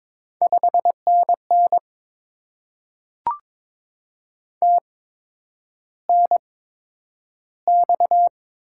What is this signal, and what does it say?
Morse code, heard as a keyed tone, about 23 wpm, spaced as for 10 wpm, 700 Hz. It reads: 5NN TNX
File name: Top 20 QSO Elements - 8x Character Spacing - 22wpm